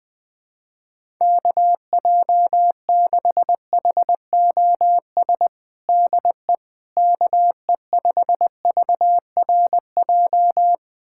Morse code, heard as a keyed tone, about 20 wpm, 700 Hz. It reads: KJ6HOS DE KE5VRJ
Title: USA Callsign DE Callsign - 6 Characters - 20wpm